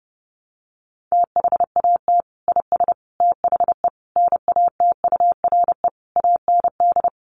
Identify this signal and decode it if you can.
Morse code, heard as a keyed tone, about 30 wpm, 700 Hz. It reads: THAT IS THE NATURE AND
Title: Sentences from Top 1100 Words 30wpm